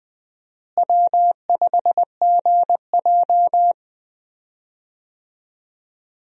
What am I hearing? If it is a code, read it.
Morse code, heard as a keyed tone, about 21 wpm, 700 Hz. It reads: W5GJ
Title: USA Call Signs - 4 Characters - Repeated 3x - 20wpm